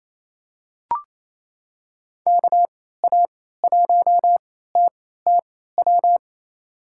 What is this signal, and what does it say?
Morse code, heard as a keyed tone, about 28 wpm, spaced as for 16 wpm, 700 Hz. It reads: KA1TTW
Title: USA Call Signs - 6 Characters - 3x Character Spacing - 28wpm